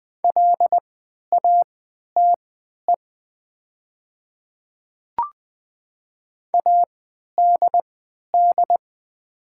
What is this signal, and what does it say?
Morse code, heard as a keyed tone, about 20 wpm, spaced as for 11 wpm, 700 Hz. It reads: LATE ADD
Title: 3x Character Spacing - Top 300 Words - 20wpm